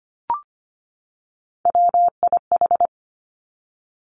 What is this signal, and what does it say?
Morse code, heard as a keyed tone, about 25 wpm, 700 Hz. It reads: WIH
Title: Three Challenging Letter-Number Combinations 25wpm